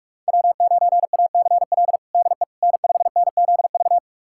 Morse code, heard as a keyed tone, about 42 wpm, 700 Hz. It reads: W9ACL DE N5NZV
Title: USA Callsign DE Callsign - 5 Characters - 45wpm